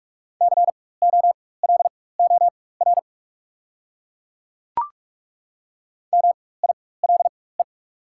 Morse code, heard as a keyed tone, about 43 wpm, spaced as for 21 wpm, 700 Hz. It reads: COLOR MILE
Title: 4x Character Spacing - Top 300 Words - 45wpm